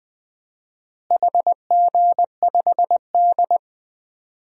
Morse code, heard as a keyed tone, about 20 wpm, 700 Hz. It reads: HG5D
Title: DXCC-100 Call Signs - 4 Characters - Repeated 2x - 20wpm